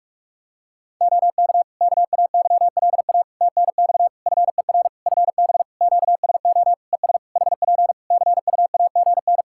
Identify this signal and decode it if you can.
Morse code, heard as a keyed tone, about 46 wpm, 700 Hz. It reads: OK KAYLA TNX FER FB QSO ES HP CUAGN
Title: Practice QSO - Exchange 4 - Ending - 45wpm